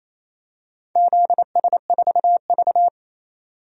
Morse code, heard as a keyed tone, about 28 wpm, 700 Hz. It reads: ZS4V
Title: DXCC-100 Call Signs - 4 Characters - 28wpm